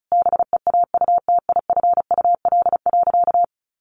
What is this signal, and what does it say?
Morse code, heard as a keyed tone, about 34 wpm, 700 Hz. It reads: BEAUTIFUL.
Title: Sentences from Top 900 Words 35wpm